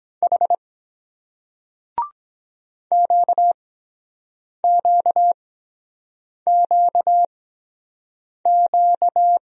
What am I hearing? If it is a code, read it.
Morse code, heard as a keyed tone, about 23 wpm, spaced as for 7 wpm, 700 Hz. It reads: H QQQQ